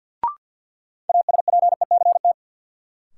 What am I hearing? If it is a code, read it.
Morse code, heard as a keyed tone, about 46 wpm, 700 Hz. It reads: ASPECT
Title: Sentences from Top 1300 Words - New Word Drill Alt - 50wpm